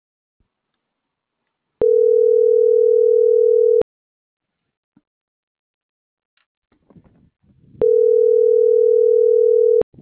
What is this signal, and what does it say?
Ringback tone (North America), 440 + 480 Hz.